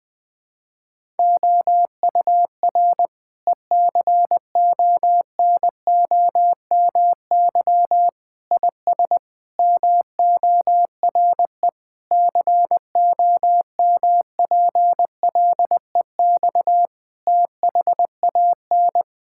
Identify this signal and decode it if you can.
Morse code, heard as a keyed tone, about 20 wpm, 700 Hz. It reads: OUR ECONOMY IS MORE COMPLEX THAN
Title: Sentences from Top 1000 Words - Review Part 6 - 20wpm